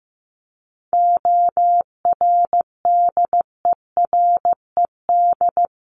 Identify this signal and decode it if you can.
Morse code, heard as a keyed tone, about 15 wpm, 700 Hz. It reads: ORDERED
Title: A Sound Of Thunder 15wpm